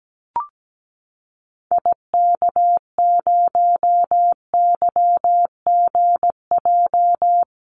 Morse code, heard as a keyed tone, about 17 wpm, 700 Hz. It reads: IK0YGJ
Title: Call Signs Repeated 2x - 17wpm